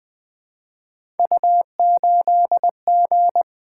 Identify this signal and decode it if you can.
Morse code, heard as a keyed tone, about 20 wpm, 700 Hz. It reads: U8G
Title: Three Letter-Number Combinations - 20wpm